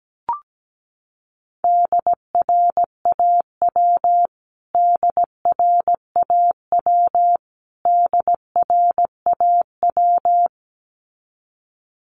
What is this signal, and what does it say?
Morse code, heard as a keyed tone, about 17 wpm, 700 Hz. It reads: DRAW DRAW DRAW
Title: Top 500 Words Repeated 17wpm